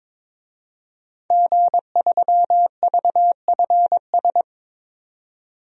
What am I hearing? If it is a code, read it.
Morse code, heard as a keyed tone, about 22 wpm, 700 Hz. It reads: G3VFS
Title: DXCC-100 Call Signs - 5 Characters - 22wpm